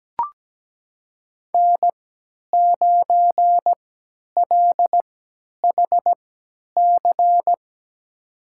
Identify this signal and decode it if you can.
Morse code, heard as a keyed tone, about 17 wpm, spaced as for 10 wpm, 700 Hz. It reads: N9LHC
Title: USA Call Signs - 5 Characters - 3x Character Spacing - 17wpm